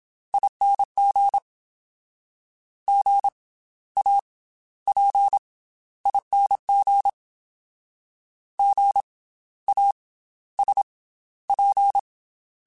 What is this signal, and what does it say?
Morse code, heard as a keyed tone, about 27 wpm, spaced as for 14 wpm, 800 Hz. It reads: ING GAPING GASP
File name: Special INGs / Special INGs-mp3s / Slow ing @25